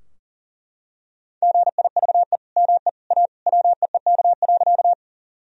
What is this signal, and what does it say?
Morse code, heard as a keyed tone, about 40 wpm, 700 Hz. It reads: GIVE ME A WEEK.
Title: Sentences from Top 1000 Words - Review Part 2 - 40wpm